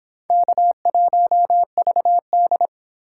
Morse code, heard as a keyed tone, about 25 wpm, 700 Hz. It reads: K1VD